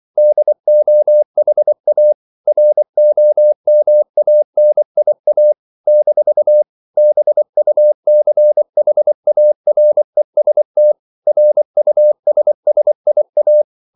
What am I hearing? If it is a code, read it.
Morse code, heard as a keyed tone, about 24 wpm, 600 Hz. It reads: DOHA ROMANIA - BUCHAREST RUSSIA
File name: CW / Practice / NationalCapitals24